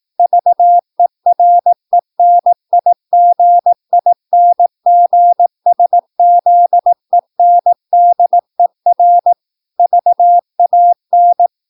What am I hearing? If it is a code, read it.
Morse code, heard as a keyed tone, about 18 wpm, 700 Hz. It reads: VERENIGINGSZENDER VAN
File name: PI4AA Uitzending